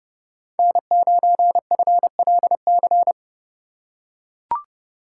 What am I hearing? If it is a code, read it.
Morse code, heard as a keyed tone, about 31 wpm, 700 Hz. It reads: N9FLC
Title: USA Call Signs - 5 Characters - 2x Character Spacing - 30wpm